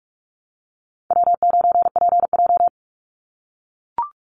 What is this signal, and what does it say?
Morse code, heard as a keyed tone, about 44 wpm, 700 Hz. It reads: W9PJ
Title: Easy Call Signs - 45wpm